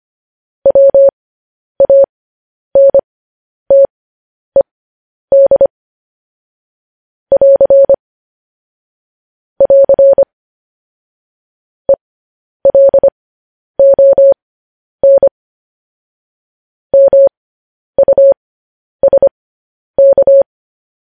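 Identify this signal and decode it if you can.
Morse code, heard as a keyed tone, about 25 wpm, spaced as for 10 wpm, 550 Hz. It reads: WANTED <AR> <AR> ELON MUSK
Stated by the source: Morse code transmission of news headlines.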